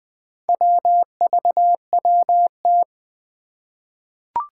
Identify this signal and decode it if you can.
Morse code, heard as a keyed tone, about 20 wpm, 700 Hz. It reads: WVWT
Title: Four Challenging Letter-Number Combinations 20wpm